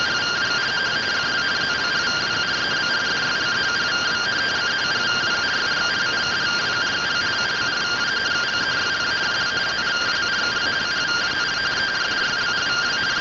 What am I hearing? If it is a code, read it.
Two-tone teleprinter signal, about 170 Hz shift.